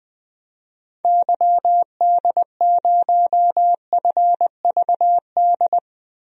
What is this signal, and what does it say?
Morse code, heard as a keyed tone, about 20 wpm, 700 Hz. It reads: YD0FVD
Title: DXCC-100 Call Signs - 6 Characters - 20wpm